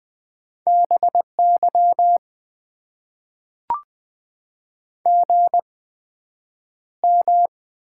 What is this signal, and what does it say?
Morse code, heard as a keyed tone, about 20 wpm, spaced as for 9 wpm, 700 Hz. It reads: BY GM
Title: Top 90 QSO Elements - 8x Character Spacing - 20wpm